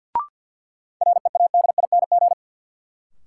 Morse code, heard as a keyed tone, about 52 wpm, 700 Hz. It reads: READING
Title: Sentences from Top 600 Words - New Word Drill - 50wpm